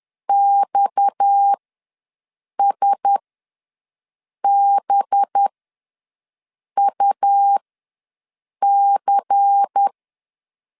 Morse code, heard as a keyed tone, about 10 wpm, spaced as for 5 wpm, 800 Hz. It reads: XSBUC
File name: LEZIONI / 7 GF9Z / 7 GF9 ws 10wpm 4034